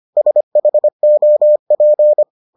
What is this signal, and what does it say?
Morse code, heard as a keyed tone, about 25 wpm, 600 Hz. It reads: SHOP